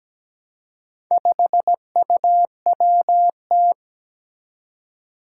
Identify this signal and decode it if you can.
Morse code, heard as a keyed tone, about 17 wpm, 700 Hz. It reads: 5UWT
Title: Four Challenging Letter-Number Combinations 17wpm